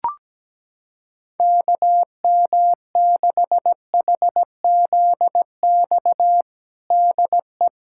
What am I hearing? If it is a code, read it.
Morse code, heard as a keyed tone, about 17 wpm, 700 Hz. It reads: KM6HZX DE
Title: USA Callsign DE Callsign 17wpm